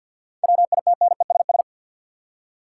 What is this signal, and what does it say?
Morse code, heard as a keyed tone, about 51 wpm, 700 Hz. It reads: WITNESS
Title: Seven Letter Words - 50wpm